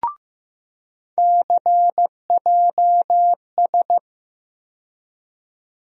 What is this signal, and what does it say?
Morse code, heard as a keyed tone, about 15 wpm, 700 Hz. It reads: CJS
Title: USA Call Sign Postfixes - 3 Letters 15wpm